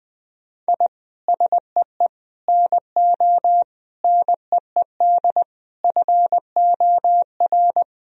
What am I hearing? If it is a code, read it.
Morse code, heard as a keyed tone, about 20 wpm, 700 Hz. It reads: I SEE NO NEED FOR
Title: Sentences from Top 1000 Words - Review Part 1 - 20wpm